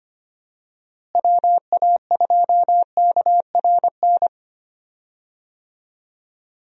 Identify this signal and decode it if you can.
Morse code, heard as a keyed tone, about 25 wpm, 700 Hz. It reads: WA2KRN
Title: USA Call Signs - 6 Characters - Repeated 2x - 25wpm